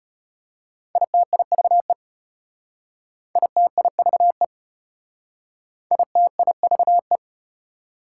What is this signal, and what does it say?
Morse code, heard as a keyed tone, about 32 wpm, 690 Hz. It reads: ITIVE ITIVE ITIVE